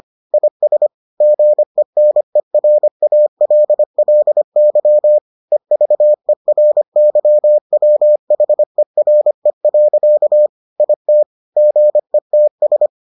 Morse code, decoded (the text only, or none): IS GENERALLY EVERYWHERE. IT GETS